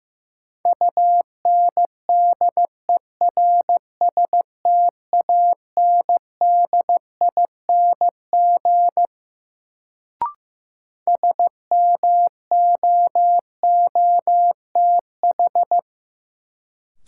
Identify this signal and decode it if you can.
Morse code, heard as a keyed tone, about 15 wpm, 700 Hz. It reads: UNDERSTANDING SMOOTH
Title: Sentences from Top 1400 Words - New Word Drill Alt - 15wpm